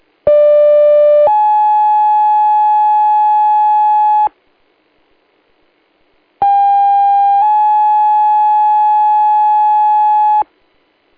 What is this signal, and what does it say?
Two-tone page, heard as a one-second tone, then a three-second tone, about 590 then 850 Hz, about 800 then 850 Hz. It is heard from 0.3 to 10.4 s.